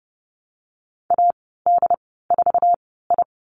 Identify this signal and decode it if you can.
Morse code, heard as a keyed tone, about 30 wpm, spaced as for 17 wpm, 700 Hz. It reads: AD4I
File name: Lesson 14g - Number 2 - Call Signs 30wpm